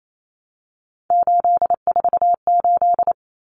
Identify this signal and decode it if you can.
Morse code, heard as a keyed tone, about 28 wpm, 700 Hz. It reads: 848
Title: Common Cut Numbers - 3-Digit 28wpm